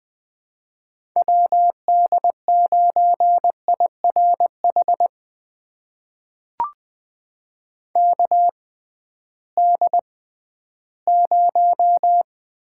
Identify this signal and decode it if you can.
Morse code, heard as a keyed tone, about 21 wpm, spaced as for 13 wpm, 700 Hz. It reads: WD9IRH KD0
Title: USA Call Signs - 6 Characters - 6x Character Spacing - 20wpm